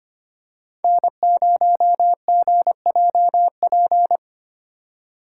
Morse code, heard as a keyed tone, about 25 wpm, 700 Hz. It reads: N0GJP
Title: USA Call Signs - 2x Character Spacing - 25wpm